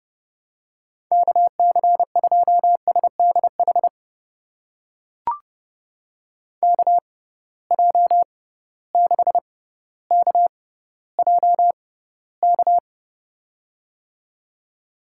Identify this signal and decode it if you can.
Morse code, heard as a keyed tone, about 32 wpm, spaced as for 16 wpm, 700 Hz. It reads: KC2SDH KJ6KJK
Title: USA Call Signs - 6 Characters - 6x Character Spacing - 30wpm